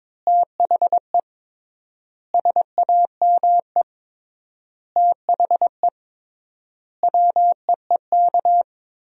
Morse code, heard as a keyed tone, about 22 wpm, 700 Hz. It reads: THE SAME THE WEEK
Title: 3x Word-Spacing - Sentences from Top 500 Words - 22wpm